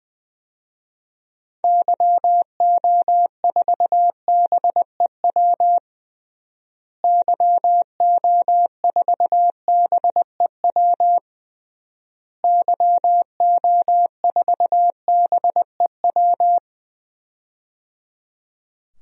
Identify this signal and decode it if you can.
Morse code, heard as a keyed tone, about 21 wpm, 700 Hz. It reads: YO4BEW YO4BEW YO4BEW
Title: DXCC-100 Call Signs - 6 Characters - Repeated 3x - 20wpm